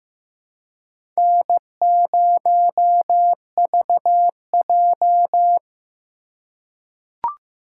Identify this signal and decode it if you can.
Morse code, heard as a keyed tone, about 15 wpm, 700 Hz. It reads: N0VJ
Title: USA Call Signs - 3x Character Spacing - 15wpm